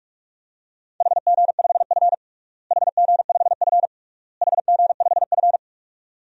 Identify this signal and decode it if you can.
Morse code, heard as a keyed tone, about 46 wpm, 700 Hz. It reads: HG5F HG5F HG5F
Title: DXCC-100 Call Signs - 4 Characters - Repeated 3x - 45wpm